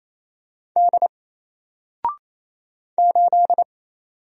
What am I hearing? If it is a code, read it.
Morse code, heard as a keyed tone, about 28 wpm, 700 Hz. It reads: D 8
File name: Single-Letter-Number-28wpm